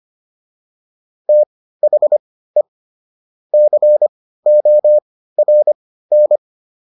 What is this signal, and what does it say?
Morse code, heard as a keyed tone, about 25 wpm, spaced as for 15 wpm, 600 Hz. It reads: THE CORN